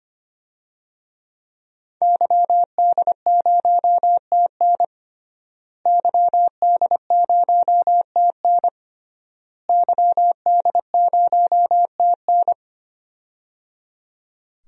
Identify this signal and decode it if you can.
Morse code, heard as a keyed tone, about 25 wpm, 700 Hz. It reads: YD0TN YD0TN YD0TN
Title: DXCC-100 Call Signs - 5 Characters - Repeated 3x - 25wpm